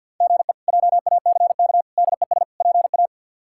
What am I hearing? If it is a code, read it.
Morse code, heard as a keyed tone, about 53 wpm, 700 Hz. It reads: GE JACK DES PA
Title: K1USN SST QSO Practice - 50wpm